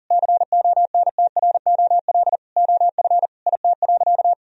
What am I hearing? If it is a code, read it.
Morse code, heard as a keyed tone, about 37 wpm, 700 Hz. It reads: CONTROL OF IT.
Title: Sentences from Top 1000 Words - Review Part 3 - 40wpm